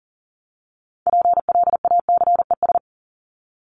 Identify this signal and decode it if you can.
Morse code, heard as a keyed tone, about 40 wpm, 700 Hz. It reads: PLACES
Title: ES:S Suffix - Top 100 Words with Base Words 40wpm